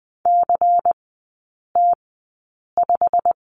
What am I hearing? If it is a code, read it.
Morse code, heard as a keyed tone, about 20 wpm, spaced as for 8 wpm, 700 Hz. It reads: CT5
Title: 093 Lesson 16e - Letter c - Three Characters 20wpm